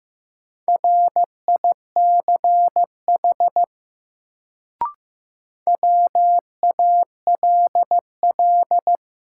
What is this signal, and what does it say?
Morse code, heard as a keyed tone, about 15 wpm, 700 Hz. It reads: RICH WALL
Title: Sentences from Top 1100 Words - New Word Drill Alt - 15wpm